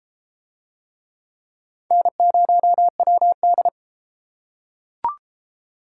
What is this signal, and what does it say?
Morse code, heard as a keyed tone, about 30 wpm, 700 Hz. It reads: N0WD